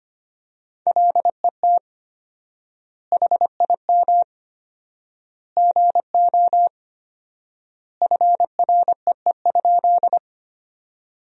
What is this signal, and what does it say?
Morse code, heard as a keyed tone, about 25 wpm, 700 Hz. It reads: LET HIM GO FREE?
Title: 4x Word-Spacing - Sentences from Top 500 Words - 25wpm